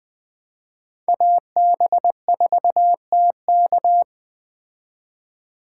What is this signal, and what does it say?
Morse code, heard as a keyed tone, about 20 wpm, 700 Hz. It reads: AB4TK
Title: Speed Racing - USA 2x2 Call Signs 17wpm